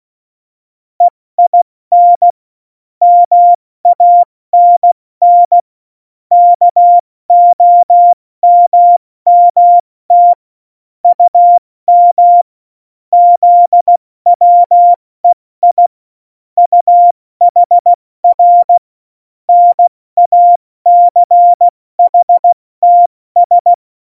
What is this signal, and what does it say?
Morse code, heard as a keyed tone, about 16 wpm, 700 Hz. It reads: EIN MANN KOMMT UM ZWEI UHR NACHTS
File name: Morsekurs / Witze / Witz 29. Oktober 2019, 70 F80